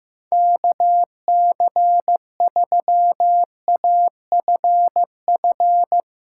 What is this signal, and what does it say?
Morse code, heard as a keyed tone, about 15 wpm, 700 Hz. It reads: KC3AFF
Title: USA Callsign DE Callsign - 6 Characters - 15wpm